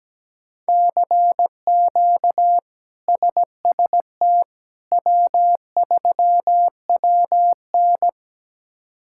Morse code, heard as a keyed tone, about 17 wpm, 700 Hz. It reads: CQ SST W3WN
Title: K1USN SST QSO Practice - 17wpm